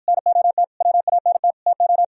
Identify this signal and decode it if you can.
Morse code, heard as a keyed tone, about 53 wpm, 700 Hz. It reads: NOT WANT TO